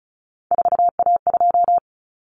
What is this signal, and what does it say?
Morse code, heard as a keyed tone, about 34 wpm, 700 Hz. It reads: 4A2
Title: Common Cut Numbers - 3-Digit 35wpm